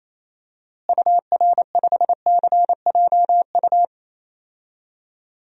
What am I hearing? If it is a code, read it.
Morse code, heard as a keyed tone, about 27 wpm, 700 Hz. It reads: UR5CJU
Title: DXCC-100 Call Signs - 6 Characters - Repeated 2x - 28wpm